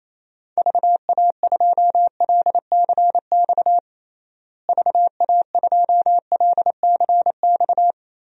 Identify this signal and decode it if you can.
Morse code, heard as a keyed tone, about 28 wpm, 700 Hz. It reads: VA2LCX VA2LCX
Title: DXCC-100 Call Signs - 6 Characters - Repeated 2x - 28wpm